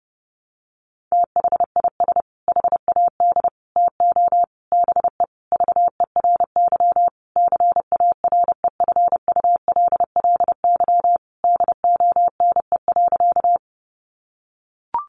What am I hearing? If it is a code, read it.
Morse code, heard as a keyed tone, about 30 wpm, 700 Hz. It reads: THIS HAD TO BE VERY CAREFULLY DONE.
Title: Sentences from Top 1300 Words 30wpm